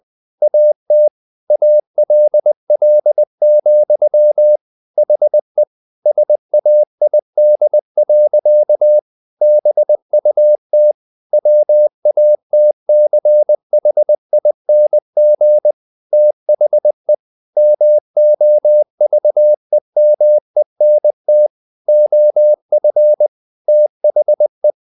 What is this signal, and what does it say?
Morse code, heard as a keyed tone, about 20 wpm, 600 Hz. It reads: AT ALL, HE SAID. BUT WATCHING THE MOVEMENT OF THE
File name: the old man and the sea part 2 20wpm